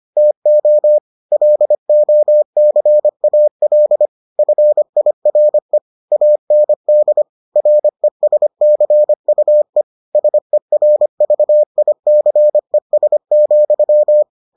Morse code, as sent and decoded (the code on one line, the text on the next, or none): - --- / .-.. --- -.-. .- .-.. / ..-. .. .-. / .- -. -.. / .-. ... -.-. ..- . / ... .-. ...- .. -.-. ... --..--
TO LOCAL FIRE AND RESCUE SERVICES,